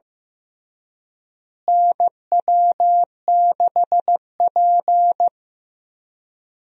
Morse code, heard as a keyed tone, about 15 wpm, 700 Hz. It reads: NW6P
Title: USA Call Signs - 4 Characters - Repeated 3x - 15wpm